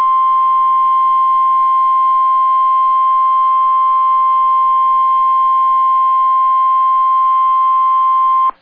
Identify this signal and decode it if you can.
Emergency Alert System (SAME) tones (the 1050 Hz attention tone). It is heard from 0.3 to 7.9 s.